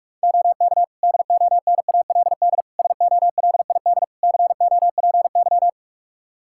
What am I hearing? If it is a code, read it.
Morse code, heard as a keyed tone, about 41 wpm, 700 Hz. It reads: OK DONALD SOLID COPY
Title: Practice QSO - Exchange 3 - More Info - 45wpm